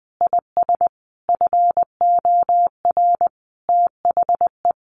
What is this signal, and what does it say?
Morse code, heard as a keyed tone, about 20 wpm, 700 Hz. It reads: IS FOR THE
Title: Sentences from Top 1500 Words 20wpm